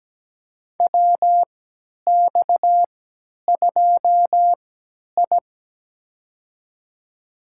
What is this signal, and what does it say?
Morse code, heard as a keyed tone, about 17 wpm, spaced as for 10 wpm, 700 Hz. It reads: WX2I
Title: USA Call Signs - 4 Characters - 3x Character Spacing - 17wpm